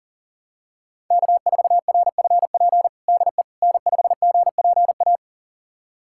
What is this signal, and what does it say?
Morse code, heard as a keyed tone, about 40 wpm, 700 Hz. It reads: K4RFP DE N5GPA
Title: USA Callsign DE Callsign - 5 Characters - 40wpm